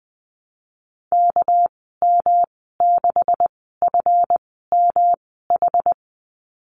Morse code, heard as a keyed tone, about 20 wpm, spaced as for 14 wpm, 700 Hz. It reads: KM6FMH
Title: 213 Lesson 34g - Number 8 - Call Signs 20wpm